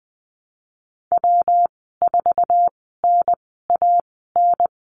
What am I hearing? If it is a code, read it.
Morse code, heard as a keyed tone, about 20 wpm, spaced as for 15 wpm, 700 Hz. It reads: W4NAN
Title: 193 Lesson 31g - Letter v - Call Signs 20wpm